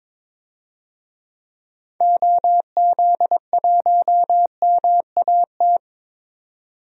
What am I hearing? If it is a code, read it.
Morse code, heard as a keyed tone, about 22 wpm, 700 Hz. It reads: OZ1MAT
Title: DXCC-100 Call Signs - 6 Characters - 22wpm